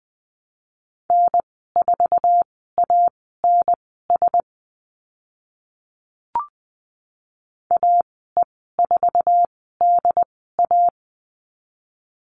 Morse code, heard as a keyed tone, about 20 wpm, spaced as for 14 wpm, 700 Hz. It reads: N4ANS AE4DA
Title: 069 Lesson 12g - Letter d - Call Signs 20wpm